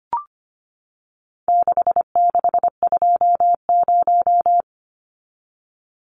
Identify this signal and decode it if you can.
Morse code, heard as a keyed tone, about 25 wpm, 700 Hz. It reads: 6620
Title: Numbers - 4-Digit 25wpm